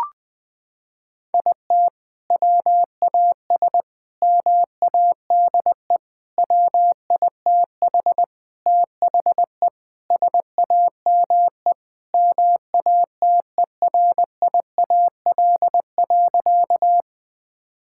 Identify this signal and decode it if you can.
Morse code, heard as a keyed tone, about 19 wpm, 700 Hz. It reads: IT WAS MADE WITH THE SAME MATERIAL.
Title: Sentences from Top 1000 Words - Review Part 5 - 20wpm